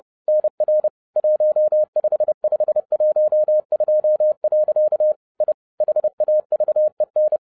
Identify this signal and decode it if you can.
Morse code, heard as a keyed tone, about 30 wpm, 610 Hz. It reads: NR 15512. I HAVEN